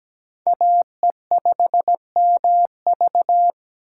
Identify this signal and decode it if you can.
Morse code, heard as a keyed tone, about 20 wpm, 700 Hz. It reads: AE5MV
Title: Speed Racing - USA 2x2 Call Signs 17wpm